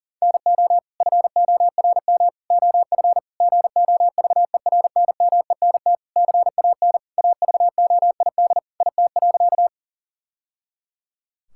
Morse code, heard as a keyed tone, about 40 wpm, 700 Hz. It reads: NO FORM OF GOVERNMENT CAN AVOID IT.
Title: Sentences from Top 1000 Words - Review Part 6 - 40wpm